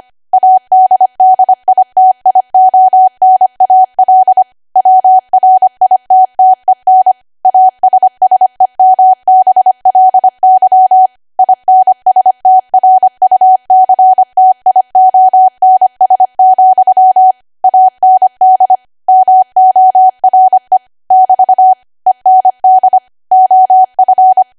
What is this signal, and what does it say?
Morse code, heard as a keyed tone, about 25 wpm, 750 Hz. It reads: ADDITIONAL WRITTEN ASSEMBLY INSTRUCTIONS, AND MORE <BT> END OF
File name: Archive / 25 WPM / 25WPM